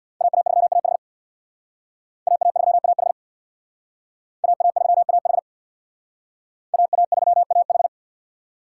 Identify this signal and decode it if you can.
Morse code, heard as a keyed tone, about 64 wpm, 700 Hz. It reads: AA3AH AA3AH AA3AH AA3AH